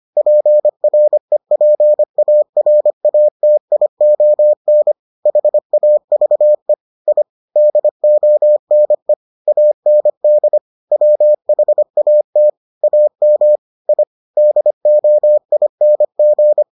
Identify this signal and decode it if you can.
Morse code, heard as a keyed tone, about 25 wpm, 600 Hz. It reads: PREPARATION HAVE I DONE AND WHAT AM I DOING